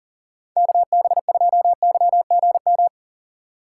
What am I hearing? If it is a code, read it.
Morse code, heard as a keyed tone, about 38 wpm, 700 Hz. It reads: KB2YGM
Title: USA Call Signs - 3x Character Spacing - 40wpm